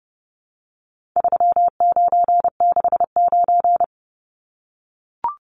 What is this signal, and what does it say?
Morse code, heard as a keyed tone, about 30 wpm, 700 Hz. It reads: 3969
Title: Numbers - 4-Digit 30wpm